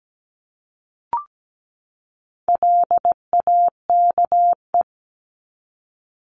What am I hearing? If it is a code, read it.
Morse code, heard as a keyed tone, about 17 wpm, 700 Hz. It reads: LAKE
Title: Top 1001-1250 Words 17wpm